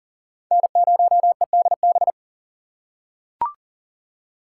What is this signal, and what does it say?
Morse code, heard as a keyed tone, about 40 wpm, 700 Hz. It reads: N0EDB
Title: USA Call Signs - 5 Characters - 2x Character Spacing - 40wpm